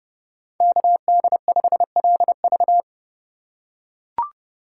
Morse code, heard as a keyed tone, about 30 wpm, 700 Hz. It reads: KD5LV
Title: USA Call Signs - 5 Characters - 2x Character Spacing - 30wpm